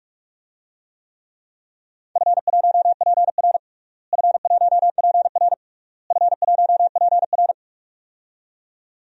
Morse code, heard as a keyed tone, about 46 wpm, 700 Hz. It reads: F1PR F1PR F1PR
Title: DXCC-100 Call Signs - 4 Characters - Repeated 3x - 45wpm